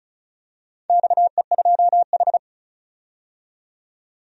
Morse code, heard as a keyed tone, about 36 wpm, 700 Hz. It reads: XE2H
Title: DXCC-100 Call Signs - 4 Characters - 35wpm